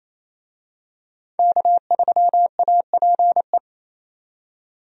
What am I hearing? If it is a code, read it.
Morse code, heard as a keyed tone, about 28 wpm, 700 Hz. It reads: K3APE
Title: USA Call Signs - 5 Characters - 28wpm